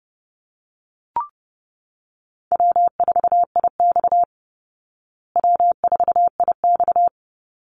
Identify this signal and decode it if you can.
Morse code, heard as a keyed tone, about 30 wpm, 700 Hz. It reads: W4IX W4IX
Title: Easy Call Signs Repeated 2x - 30wpm